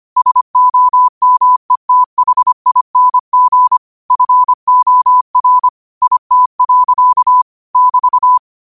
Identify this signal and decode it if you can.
Morse code, heard as a keyed tone, about 25 wpm, 1000 Hz. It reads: SOMETHING FOR IT. <BT>
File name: qotdfiles / 25wpm